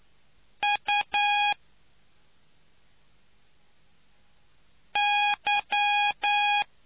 Morse code, heard as a keyed tone, about 9 wpm, spaced as for 2 wpm, 820 Hz. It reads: UY